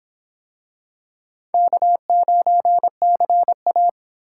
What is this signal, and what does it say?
Morse code, heard as a keyed tone, about 26 wpm, 700 Hz. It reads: K9CA